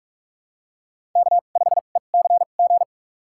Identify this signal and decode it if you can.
Morse code, heard as a keyed tone, about 46 wpm, spaced as for 32 wpm, 700 Hz. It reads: K5ECG
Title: USA Call Signs - 2x Character Spacing - 45wpm